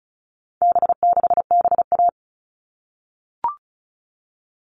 Morse code, heard as a keyed tone, about 34 wpm, 700 Hz. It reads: B6BA
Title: All Cut Numbers - 4-Digit 35wpm